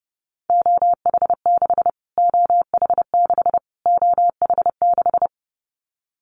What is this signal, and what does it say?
Morse code, heard as a keyed tone, about 28 wpm, 700 Hz. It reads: OH6 OH6 OH6
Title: Callsign Prefixes for Top 25 DXCC - Repeated 3x - 30wpm